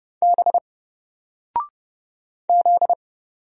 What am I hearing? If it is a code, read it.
Morse code, heard as a keyed tone, about 31 wpm, 700 Hz. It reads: B Z
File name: Single-Letter-Number-30wpm